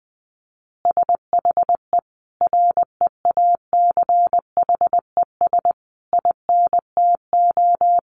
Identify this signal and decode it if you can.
Morse code, heard as a keyed tone, about 20 wpm, 700 Hz. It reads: SHE REACHES INTO
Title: Additional Easy Phrases 20wpm